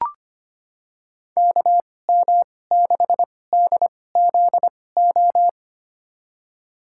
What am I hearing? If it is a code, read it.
Morse code, heard as a keyed tone, about 25 wpm, spaced as for 18 wpm, 700 Hz. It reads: KM6DZO
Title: USA Call Signs - 2x Character Spacing - 25wpm